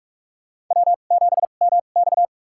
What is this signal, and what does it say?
Morse code, heard as a keyed tone, about 46 wpm, spaced as for 32 wpm, 700 Hz. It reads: W7MX
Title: USA Call Signs - 4 Characters - 2x Character Spacing - 45wpm